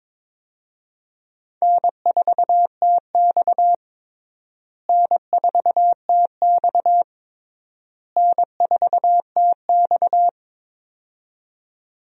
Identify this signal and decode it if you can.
Morse code, heard as a keyed tone, about 22 wpm, 700 Hz. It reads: N4TX N4TX N4TX
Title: USA Call Signs - 4 Characters - Repeated 3x - 22wpm